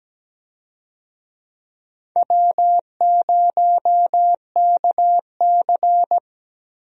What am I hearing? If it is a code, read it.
Morse code, heard as a keyed tone, about 17 wpm, 700 Hz. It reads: W0KC
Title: USA Call Signs - 4 Characters - Repeated 3x - 17wpm